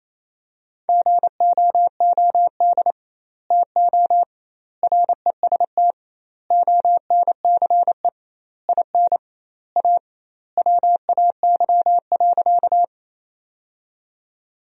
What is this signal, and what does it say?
Morse code, heard as a keyed tone, about 28 wpm, 700 Hz. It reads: GOOD TO REST ONCE IN A WAY.
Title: 2x Word-Spacing - Sentences from Top 500 Words - 28wpm